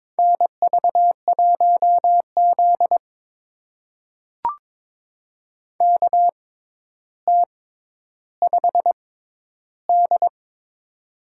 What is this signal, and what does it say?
Morse code, heard as a keyed tone, about 22 wpm, spaced as for 11 wpm, 700 Hz. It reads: NV1Z KT5D
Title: USA Call Signs - 4 Characters - 6x Character Spacing - 22wpm